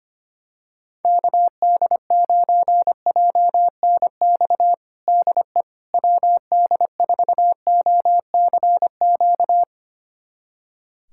Morse code, heard as a keyed tone, about 25 wpm, 700 Hz. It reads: KD9JNX DE WD4OCQ
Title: USA Callsign DE Callsign 25wpm